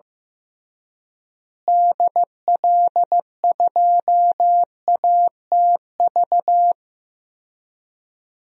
Morse code, heard as a keyed tone, about 15 wpm, 700 Hz. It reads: DL2ATV
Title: DXCC-100 Call Signs - 6 Characters - Repeated 3x - 15wpm